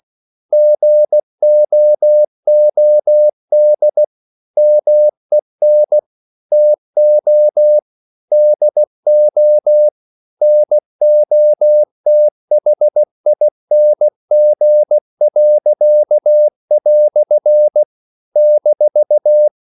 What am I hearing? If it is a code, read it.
Morse code, heard as a keyed tone, about 16 wpm, 600 Hz. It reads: GOOD MEN TO DO NOTHING." -
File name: CW / Practice / EnglishQuotations0416